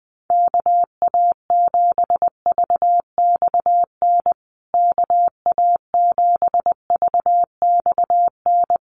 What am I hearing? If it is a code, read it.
Morse code, heard as a keyed tone, about 20 wpm, 700 Hz. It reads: KA7VXN KA7VXN
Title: USA Call Signs Repeated 3x 20wpm